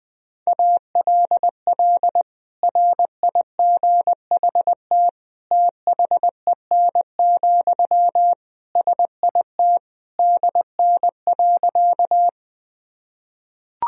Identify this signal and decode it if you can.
Morse code, heard as a keyed tone, about 20 wpm, 700 Hz. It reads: ALL RIGHT THEN, SIT DN.
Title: Sentences from Top 500 Words and Abbreviations 20wpm